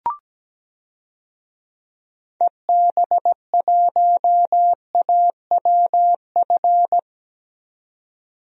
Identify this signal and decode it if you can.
Morse code, heard as a keyed tone, about 17 wpm, 700 Hz. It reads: EB1AWF
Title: DXCC-100 Call Signs - 6 Characters - 17wpm